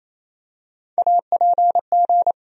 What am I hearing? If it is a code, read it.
Morse code, heard as a keyed tone, about 27 wpm, 700 Hz. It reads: APG
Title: USA Call Sign Postfixes - 3 Letters 28wpm